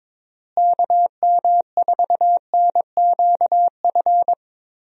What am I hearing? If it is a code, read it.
Morse code, heard as a keyed tone, about 21 wpm, 700 Hz. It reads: KM4NQF
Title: USA Call Signs - 6 Characters - Repeated 3x - 22wpm